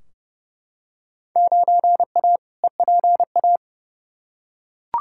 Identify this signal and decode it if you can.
Morse code, heard as a keyed tone, about 30 wpm, 700 Hz. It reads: 9A EPA
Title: Field Day Exchanges Encore - 30wpm